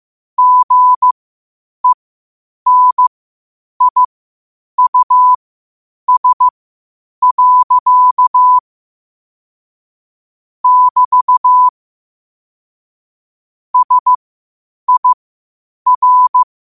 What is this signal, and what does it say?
Morse code, heard as a keyed tone, about 15 wpm, spaced as for 8 wpm, 1000 Hz. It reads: GENIUS. <BT> SIR
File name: qotdfiles / 8wpm